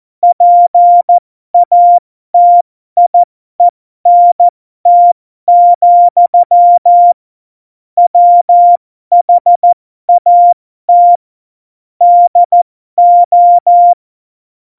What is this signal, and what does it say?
Morse code, heard as a keyed tone, about 14 wpm, spaced as for 12 wpm, 700 Hz. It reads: PATIENT, WHAT DO
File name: Morsekurs / Witze